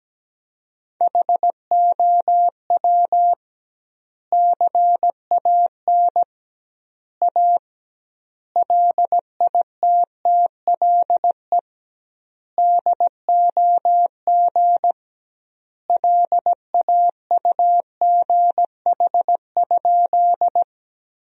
Morse code, heard as a keyed tone, about 17 wpm, 700 Hz. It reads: HOW CAN A LITTLE DOG LAUGH?
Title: 2x Word-Spacing - Sentences from Top 500 Words - 17wpm